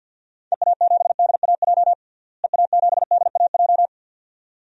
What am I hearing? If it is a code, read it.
Morse code, heard as a keyed tone, about 46 wpm, 700 Hz. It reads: EA7DAJ EA7DAJ
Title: DXCC-100 Call Signs - 6 Characters - Repeated 2x - 50wpm